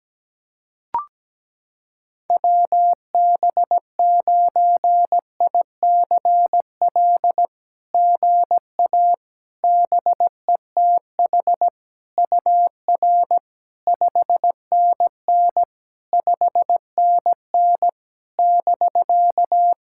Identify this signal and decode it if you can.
Morse code, heard as a keyed tone, about 17 wpm, 700 Hz. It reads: WB9ICL GA BETH UR 5NN 5NN <BK>
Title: SOTA QSO Practice 17wpm